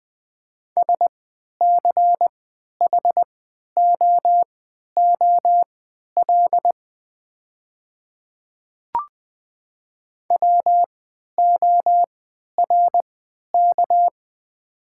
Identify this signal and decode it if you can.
Morse code, heard as a keyed tone, about 20 wpm, spaced as for 11 wpm, 700 Hz. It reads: SCHOOL WORK
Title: 3x Character Spacing - Top 300 Words - 20wpm